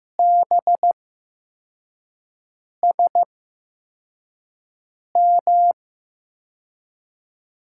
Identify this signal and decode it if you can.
Morse code, heard as a keyed tone, about 15 wpm, spaced as for 4 wpm, 700 Hz. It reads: BSM
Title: USA Call Signs - 6 Characters - 8x Character Spacing - 15wpm